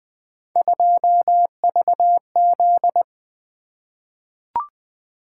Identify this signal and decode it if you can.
Morse code, heard as a keyed tone, about 20 wpm, 700 Hz. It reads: 2VZ
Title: Three Letter-Number Combinations - 20wpm